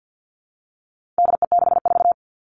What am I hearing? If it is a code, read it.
Morse code, heard as a keyed tone, about 50 wpm, 700 Hz. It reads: DE64
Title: All Cut Numbers - 4-Digit 50wpm